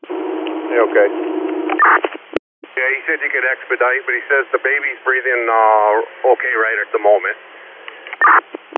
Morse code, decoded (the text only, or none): none